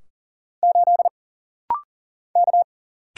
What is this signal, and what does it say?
Morse code, heard as a keyed tone, about 38 wpm, 700 Hz. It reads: 8 K
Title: Single Letter-Number Alt - Rapid-Fire 40wpm